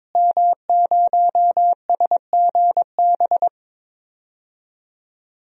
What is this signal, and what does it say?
Morse code, heard as a keyed tone, about 22 wpm, 700 Hz. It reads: M0SGB
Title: DXCC-100 Call Signs - 5 Characters - Repeated 3x - 22wpm